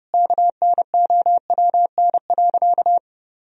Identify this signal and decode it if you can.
Morse code, heard as a keyed tone, about 32 wpm, 700 Hz. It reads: KNOWN.
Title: Sentences from Top 1000 Words - Review Part 4 - 30wpm